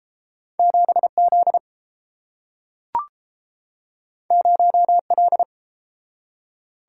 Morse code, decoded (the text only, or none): 7 Z 0 L